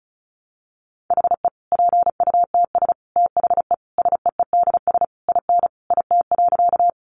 Morse code, heard as a keyed tone, about 34 wpm, 700 Hz. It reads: HE PUTS THE SEEDS IN IT.
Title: Easy Phrases 35wpm